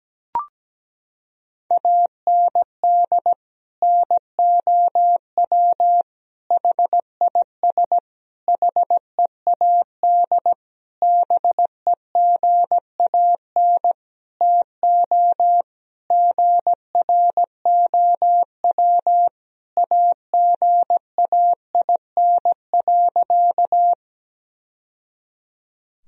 Morse code, heard as a keyed tone, about 17 wpm, 700 Hz. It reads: AND NOW HIS HEAD BEGAN TO GROW AGAIN.
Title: Sentences from Top 1000 Words - Review Part 1 - 17wpm